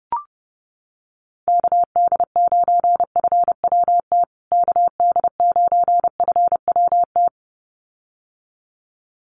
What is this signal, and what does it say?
Morse code, heard as a keyed tone, about 30 wpm, 700 Hz. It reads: KD9FWT KD9FWT
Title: USA Call Signs Repeated 2x 30wpm